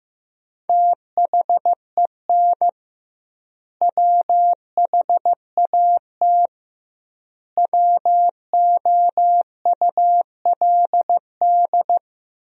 Morse code, heard as a keyed tone, about 15 wpm, 700 Hz. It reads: THEN WHAT WOULD
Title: 2x Word-Spacing - Sentences from Top 400 Words - 15wpm